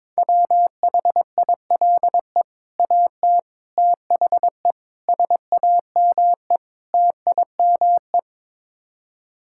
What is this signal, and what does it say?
Morse code, heard as a keyed tone, about 22 wpm, 700 Hz. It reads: WHILE AT THE SAME TIME
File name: practice / 22wpm / Build A Phrase - 5-Word Phrases - 22wpm